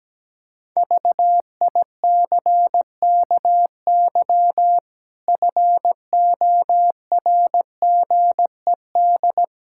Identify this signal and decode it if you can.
Morse code, heard as a keyed tone, about 17 wpm, 700 Hz. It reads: VICKY FORGED
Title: Pangram Challenge v2 - 17wpm